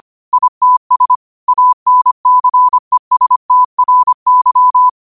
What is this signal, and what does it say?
Morse code, heard as a keyed tone, about 25 wpm, 1000 Hz. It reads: ITS ANCESTRY